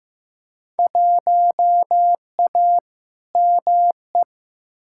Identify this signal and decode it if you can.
Morse code, heard as a keyed tone, about 15 wpm, 700 Hz. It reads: 1A ME
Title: Field Day Exchanges Encore - 15wpm